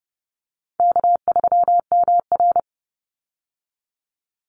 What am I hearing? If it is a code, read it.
Morse code, heard as a keyed tone, about 30 wpm, 700 Hz. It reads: K3MR
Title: Easy Call Signs Repeated 3x - 30wpm